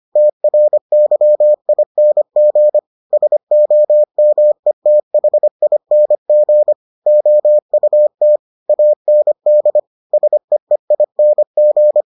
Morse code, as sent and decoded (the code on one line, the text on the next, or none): - .-. -.-- .. -. --. / ... --- -- . - .... .. -. --. / --- ..- - / .- -. -.. / ... . .. -. --.
TRYING SOMETHING OUT AND SEEING